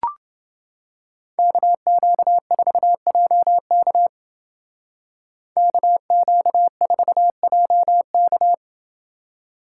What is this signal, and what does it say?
Morse code, heard as a keyed tone, about 30 wpm, 700 Hz. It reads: KQ4JK KQ4JK